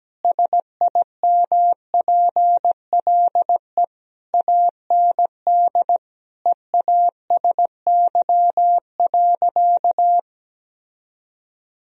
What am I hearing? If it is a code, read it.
Morse code, heard as a keyed tone, about 17 wpm, 700 Hz. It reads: SIMPLE AND EASY.
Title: Sentences from Top 1000 Words - Review Part 3 - 17wpm